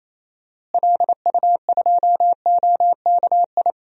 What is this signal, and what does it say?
Morse code, heard as a keyed tone, about 28 wpm, 700 Hz. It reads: LU2OKI
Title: DXCC-100 Call Signs - 6 Characters - Repeated 3x - 28wpm